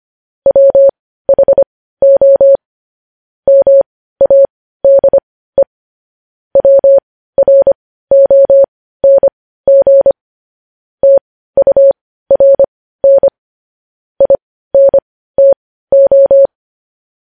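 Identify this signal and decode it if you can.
Morse code, heard as a keyed tone, about 25 wpm, spaced as for 15 wpm, 550 Hz. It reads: WHO MADE WRONG TURN INTO
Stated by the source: Morse code practice of news headlines at 15 wpm, opening with vvv.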